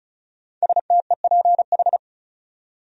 Morse code, heard as a keyed tone, about 36 wpm, 700 Hz. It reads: STEPH
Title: USA Common Names - 35wpm